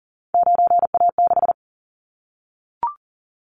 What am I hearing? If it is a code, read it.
Morse code, heard as a keyed tone, about 41 wpm, 700 Hz. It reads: 9A6
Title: Callsign Prefixes for Top 100 DXCC - 40wpm